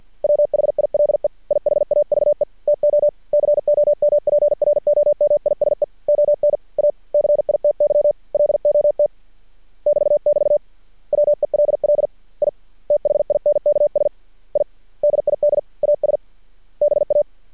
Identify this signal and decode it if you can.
Morse code, heard as a keyed tone, about 48 wpm, 590 Hz. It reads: WHILE I HAVE TO COMPROMISE ON A CITY LOT <BT><BT> WELL I THINKS I DID AS BA
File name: archived / CRSnet